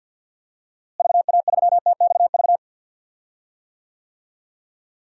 Morse code, heard as a keyed tone, about 52 wpm, 700 Hz. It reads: VA3TXV
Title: DXCC-100 Call Signs - 6 Characters - Repeated 2x - 50wpm